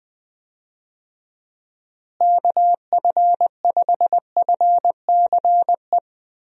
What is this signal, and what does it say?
Morse code, heard as a keyed tone, about 20 wpm, 700 Hz. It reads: KF5FCE
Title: USA Call Signs - 6 Characters - Repeated 2x - 20wpm